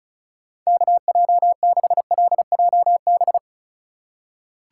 Morse code, heard as a keyed tone, about 36 wpm, 700 Hz. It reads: KJ6LJB
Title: USA Call Signs - 6 Characters - 3x Character Spacing - 35wpm